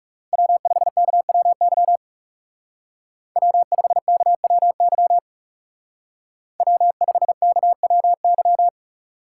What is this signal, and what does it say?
Morse code, heard as a keyed tone, about 36 wpm, 700 Hz. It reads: W5KWY W5KWY W5KWY
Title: Speed Racing - USA Call Signs 30wpm